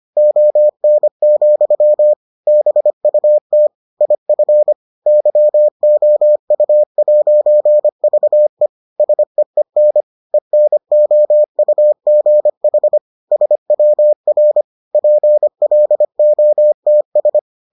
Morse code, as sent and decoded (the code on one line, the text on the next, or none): --- -. --..-- / -... ..- - / .. ..-. / -.-- --- ..- .----. ...- . / ... . -. / . -. --- ..- --. .... / ... .-- .-. / .--. .-.. --- - ...
ON, BUT IF YOU'VE SEEN ENOUGH SWR PLOTS